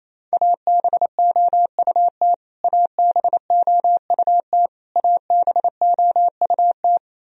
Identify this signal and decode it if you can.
Morse code, heard as a keyed tone, about 28 wpm, 700 Hz. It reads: ABOUT ABOUT ABOUT
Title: Top 100 Words Repeated 28wpm